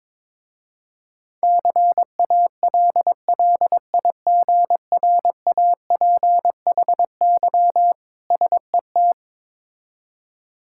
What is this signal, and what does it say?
Morse code, heard as a keyed tone, about 22 wpm, 700 Hz. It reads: CALLIGRAPHY SET
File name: Difficult Words with Hints - 22wpm